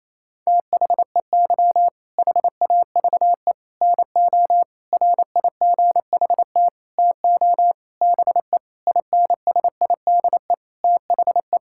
Morse code, heard as a keyed tone, about 27 wpm, 700 Hz. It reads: THEY HAVE NO RIGHT TO BE INSIDE THE
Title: Sentences from Top 1000 Words - Review Part 6 - 28wpm